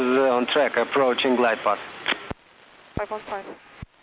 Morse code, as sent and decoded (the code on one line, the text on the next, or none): none